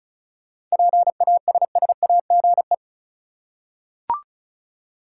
Morse code, heard as a keyed tone, about 35 wpm, 700 Hz. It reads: PASSAGE
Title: Seven Letter Words - 35wpm